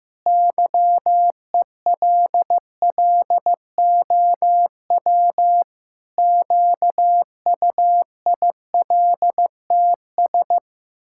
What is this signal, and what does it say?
Morse code, heard as a keyed tone, about 15 wpm, 700 Hz. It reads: YELLOW QUILTS
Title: Pangram Challenge v2 - 15wpm